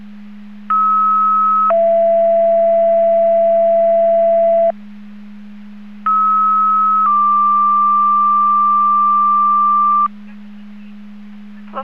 Two-tone page, heard as a one-second tone, then a three-second tone, about 1290 then 670 Hz, about 1290 then 1180 Hz. It is heard from 0.7 to 10.1 s.